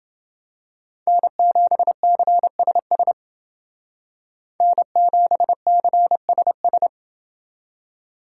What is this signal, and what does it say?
Morse code, heard as a keyed tone, about 28 wpm, 700 Hz. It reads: N7CSS N7CSS